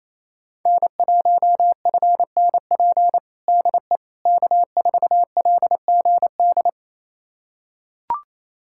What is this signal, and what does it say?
Morse code, heard as a keyed tone, about 28 wpm, 700 Hz. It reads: N1FNP DE K4LGD
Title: USA Callsign DE Callsign - 5 Characters - 28wpm